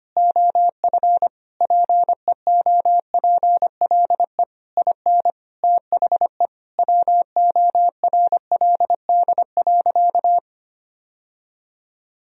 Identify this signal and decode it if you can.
Morse code, heard as a keyed tone, about 25 wpm, 700 Hz. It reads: OF PEOPLE IN THE WORLD.
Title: Sentences from Top 1000 Words - Review Part 4 - 25wpm